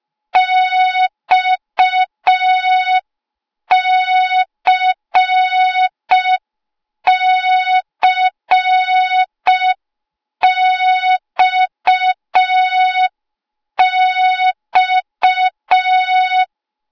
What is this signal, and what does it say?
Morse code, heard as a keyed tone, about 5 wpm, 3700 Hz. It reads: XCCXX
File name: Morze-HA2MN / hang / 08KXC HA2MN